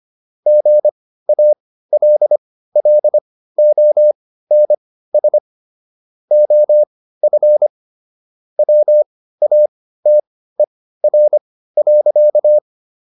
Morse code, heard as a keyed tone, about 25 wpm, spaced as for 15 wpm, 600 Hz. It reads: GALLONS OF WATER.